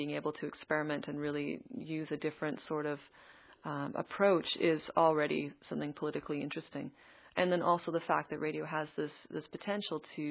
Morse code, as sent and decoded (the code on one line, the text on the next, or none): none